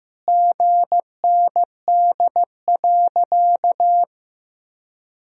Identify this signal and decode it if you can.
Morse code, heard as a keyed tone, about 15 wpm, 700 Hz. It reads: GND.
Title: Sentences from Top 500 Words and Abbreviations 15wpm